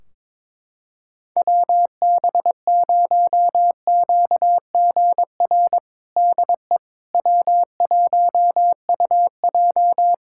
Morse code, heard as a keyed tone, about 22 wpm, 700 Hz. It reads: WB0QGR DE W1UJ
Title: Practice QSO - Exchange 4 - Ending - 22wpm